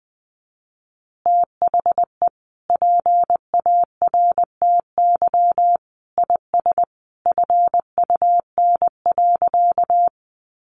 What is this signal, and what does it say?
Morse code, heard as a keyed tone, about 20 wpm, 700 Hz. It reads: THE PARTY IS FUN.
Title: Easy Phrases 20wpm